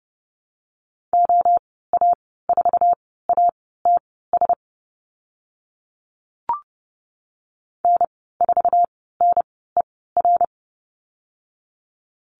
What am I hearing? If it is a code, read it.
Morse code, heard as a keyed tone, about 31 wpm, spaced as for 17 wpm, 690 Hz. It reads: OA4ATS N4NER
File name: Lesson 15g - Number 5 - Call Signs 30wpm